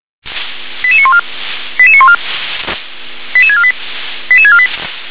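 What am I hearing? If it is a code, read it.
Five-tone selective call (ZVEI), heard as quick runs of five tones, tones of about 70 ms. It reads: 88914 88914 88648 88648
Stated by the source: ZVEI five-tone selective call received on an SDR.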